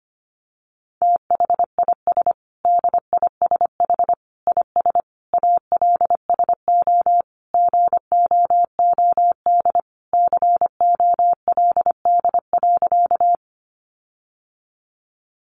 Morse code, heard as a keyed tone, about 25 wpm, 700 Hz. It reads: THIS DISH IS ALSO GOOD COLD.
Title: Sentences from Top 1300 Words 25wpm